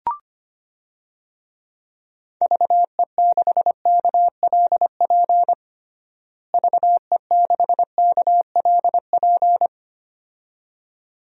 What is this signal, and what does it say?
Morse code, heard as a keyed tone, about 25 wpm, 700 Hz. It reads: VE6KLP VE6KLP
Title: DXCC-100 Call Signs - 6 Characters - Repeated 2x - 25wpm